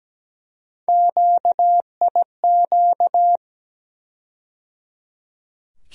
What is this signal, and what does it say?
Morse code, heard as a keyed tone, about 17 wpm, 700 Hz. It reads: QIQ
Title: USA Call Sign Postfixes - 3 Letters 17wpm